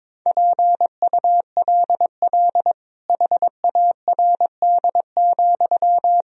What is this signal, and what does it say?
Morse code, heard as a keyed tone, about 22 wpm, 700 Hz. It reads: PULL HARD,
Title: Sentences from Top 1000 Words - Review Part 2 - 22wpm